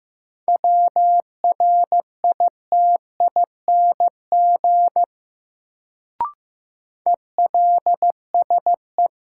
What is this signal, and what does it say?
Morse code, heard as a keyed tone, about 15 wpm, 700 Hz. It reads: WRITING ELSE
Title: Sentences from Top 800 Words - New Word Drill - 15wpm